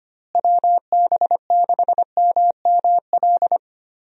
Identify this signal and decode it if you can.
Morse code, heard as a keyed tone, about 25 wpm, 700 Hz. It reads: WB6MML
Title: USA Call Signs - 6 Characters - Repeated 3x - 25wpm